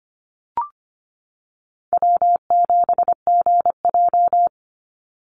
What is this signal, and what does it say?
Morse code, heard as a keyed tone, about 25 wpm, 700 Hz. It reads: W7GJ
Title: NA SOTA Chaser Callsigns 25wpm